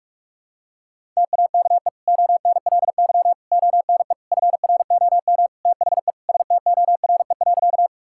Morse code, heard as a keyed tone, about 46 wpm, 700 Hz. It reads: TAKE ONLY ONE FROM THE STORE.
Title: Sentences from Top 1000 Words - Review Part 5 - 45wpm